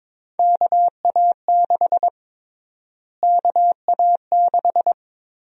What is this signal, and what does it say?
Morse code, heard as a keyed tone, about 22 wpm, 700 Hz. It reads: KA6 KA6
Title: USA Call Sign Prefixes - Repeated 2x - 22wpm